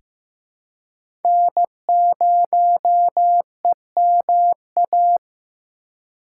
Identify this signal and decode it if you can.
Morse code, heard as a keyed tone, about 15 wpm, 700 Hz. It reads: N0EMA
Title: USA Call Signs - 5 Characters - 2x Character Spacing - 15wpm